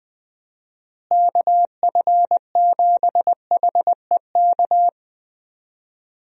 Morse code, heard as a keyed tone, about 20 wpm, 700 Hz. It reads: KF7HEK
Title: USA Call Signs - 6 Characters - Repeated 3x - 20wpm